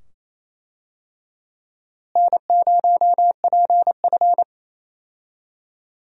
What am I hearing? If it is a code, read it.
Morse code, heard as a keyed tone, about 28 wpm, 700 Hz. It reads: N0PF
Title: USA Call Signs - 4 Characters - Repeated 2x - 28wpm